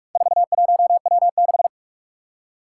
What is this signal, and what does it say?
Morse code, heard as a keyed tone, about 46 wpm, 700 Hz. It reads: 41W6
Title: Four Challenging Letter-Number Combinations 45wpm